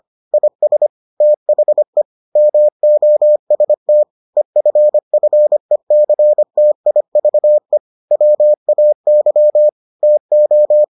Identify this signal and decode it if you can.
Morse code, heard as a keyed tone, about 25 wpm, 600 Hz. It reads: IS THE MOST EFFECTIVE WAY TO